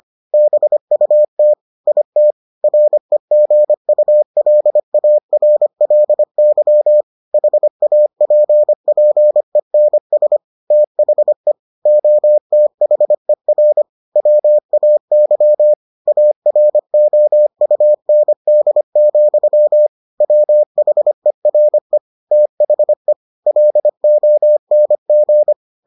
Morse code, heard as a keyed tone, about 25 wpm, 600 Hz. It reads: BUT IT REGULARLY HAPPENS THE OTHER WAY AROUND, WHERE THE LONG